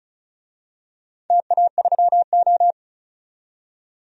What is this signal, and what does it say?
Morse code, heard as a keyed tone, about 36 wpm, 690 Hz. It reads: TA3O
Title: DXCC-100 Call Signs - 4 Characters - 35wpm